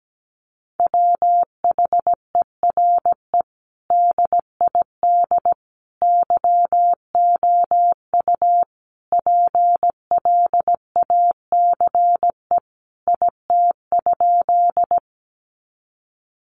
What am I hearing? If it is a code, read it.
Morse code, heard as a keyed tone, about 17 wpm, 700 Hz. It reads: WHERE DID YOU PLACE IT?
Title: Sentences from Top 200 Words 17wpm